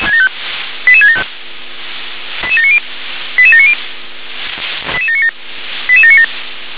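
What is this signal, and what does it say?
Five-tone selective call (ZVEI), heard as quick runs of five tones, tones of about 70 ms. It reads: 88675 88790 88790 88797 88797
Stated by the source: ZVEI five-tone selective call received on an SDR.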